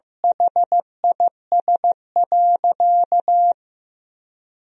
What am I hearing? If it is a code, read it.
Morse code, heard as a keyed tone, about 16 wpm, 700 Hz. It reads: HIS.
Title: Sentences from Top 1000 Words - Review Part 3 - 15wpm